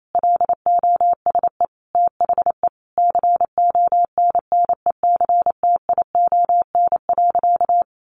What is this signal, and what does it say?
Morse code, heard as a keyed tone, about 28 wpm, 700 Hz. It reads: LOSE THE CONNECTION.
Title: Sentences from Top 1300 Words 28wpm